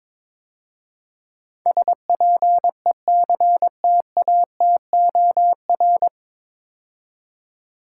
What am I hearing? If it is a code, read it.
Morse code, heard as a keyed tone, about 22 wpm, 700 Hz. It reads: SPECTATOR
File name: 9 Letter Words - First 3 Letters Given - 22wpm